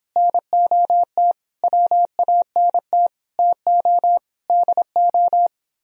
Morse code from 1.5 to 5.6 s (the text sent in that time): WANT TO DO